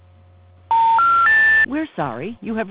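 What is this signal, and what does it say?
Special information tone, with speech.
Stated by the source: Outbound phone call reaching an automatic voicemail system and a disconnected-number intercept message.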